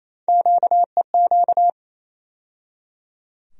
Morse code, heard as a keyed tone, about 28 wpm, 700 Hz. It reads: QEQ
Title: USA Call Sign Postfixes - 3 Letters 28wpm